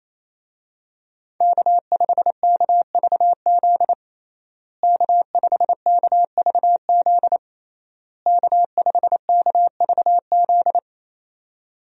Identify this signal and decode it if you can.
Morse code, heard as a keyed tone, about 28 wpm, 700 Hz. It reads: K5KVZ K5KVZ K5KVZ
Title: USA Call Signs - 5 Characters - Repeated 3x - 28wpm